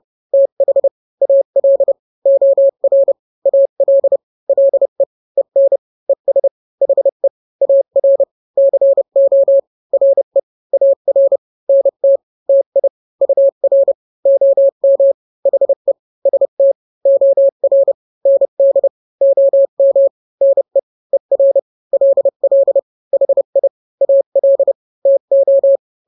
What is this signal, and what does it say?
Morse code, heard as a keyed tone, about 30 wpm, 550 Hz. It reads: TH AL OR AL LE EN ES HE AR CO RE AR NT TI UR OM HE ST OR ND OM NE ER LL HI AL TO
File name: n-grams / 2-grams 30 25 WPM